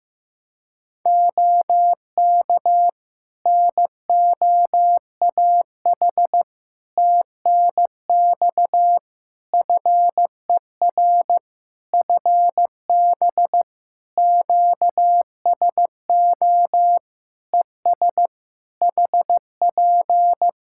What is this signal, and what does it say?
Morse code, heard as a keyed tone, about 15 wpm, 700 Hz. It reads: OK NOAH TNX FER FB QSO ES HP
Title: Practice QSO - Exchange 4 - Ending - 15wpm